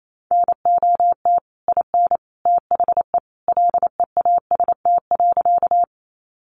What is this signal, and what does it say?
Morse code, heard as a keyed tone, about 28 wpm, 700 Hz. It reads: NOT IN THE LEAST.
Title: Sentences from Top 700 Words 28wpm